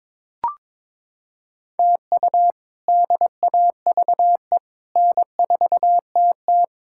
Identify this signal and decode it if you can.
Morse code, heard as a keyed tone, about 22 wpm, 700 Hz. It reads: TU DAVE N4TT
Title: K1USN SST QSO Practice - 22wpm